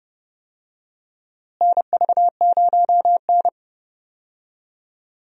Morse code, heard as a keyed tone, about 31 wpm, 700 Hz. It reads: NV0N
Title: USA Call Signs - 4 Characters - Repeated 3x - 30wpm